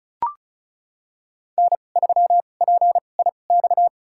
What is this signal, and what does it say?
Morse code, heard as a keyed tone, about 36 wpm, spaced as for 25 wpm, 700 Hz. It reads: N3PIX
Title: USA Call Signs - 5 Characters - 2x Character Spacing - 35wpm